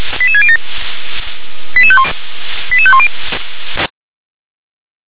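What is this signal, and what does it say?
Five-tone selective call (ZVEI), heard as quick runs of five tones, tones of about 70 ms. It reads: 88607 88510
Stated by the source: ZVEI five-tone call received by SDR.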